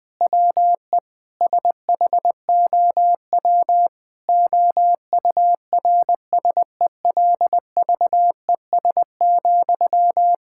Morse code, heard as a keyed tone, about 19 wpm, 700 Hz. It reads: WE SHOW OURSELVES,
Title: Sentences from Top 1600 Words 20wpm